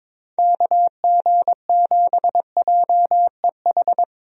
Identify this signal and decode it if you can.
Morse code, heard as a keyed tone, about 22 wpm, 700 Hz. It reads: KG7JEH
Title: USA Call Signs - 6 Characters - 22wpm